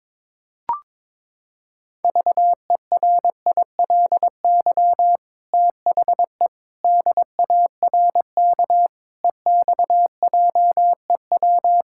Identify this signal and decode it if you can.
Morse code, heard as a keyed tone, about 22 wpm, 700 Hz. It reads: VERILY THE DARK EXJEW
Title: Pangram Challenge v2 - 22wpm